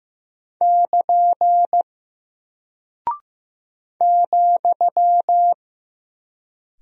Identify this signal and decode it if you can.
Morse code, heard as a keyed tone, about 15 wpm, 700 Hz. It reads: <KN> ,
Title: Single Character (Letter:Number:Punctuation:Prosign) 15wpm